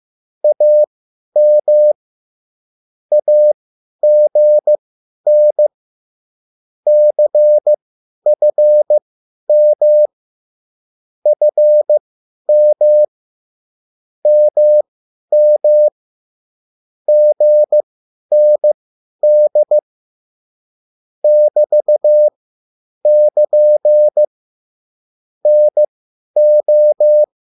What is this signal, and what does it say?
Morse code, heard as a keyed tone, about 15 wpm, spaced as for 10 wpm, 600 Hz. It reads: AM AGN CFM FM MM GND <BT><KN> NO
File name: cwbook